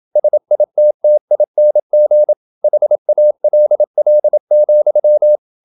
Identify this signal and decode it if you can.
Morse code, heard as a keyed tone, about 27 wpm, 600 Hz. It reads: SITTING HALL,
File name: Never 0027wpm